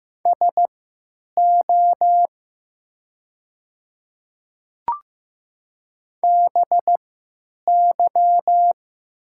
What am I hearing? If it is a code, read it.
Morse code, heard as a keyed tone, about 15 wpm, spaced as for 9 wpm, 700 Hz. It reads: SO BY
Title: 3x Character Spacing - Top 200 Words - 15wpm